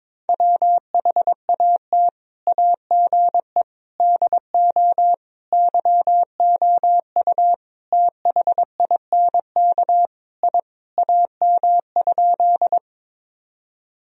Morse code, heard as a keyed tone, about 22 wpm, 700 Hz. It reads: WHAT AGE DO YOU THINK I AM?
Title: Sentences from Top 1000 Words - Review Part 3 - 22wpm